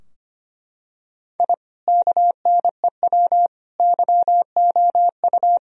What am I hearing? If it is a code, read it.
Morse code, heard as a keyed tone, about 25 wpm, 700 Hz. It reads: I KNEW YOU
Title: Sentences from Top 1000 Words - Review Part 2 - 25wpm